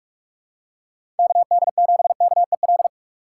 Morse code, heard as a keyed tone, about 46 wpm, 700 Hz. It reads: KD7KEL
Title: USA Call Signs - 6 Characters - 4x Character Spacing - 45wpm